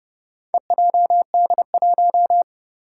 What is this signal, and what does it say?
Morse code, heard as a keyed tone, about 32 wpm, 700 Hz. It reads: EJD1
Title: Four Challenging Letter-Number Combinations 30wpm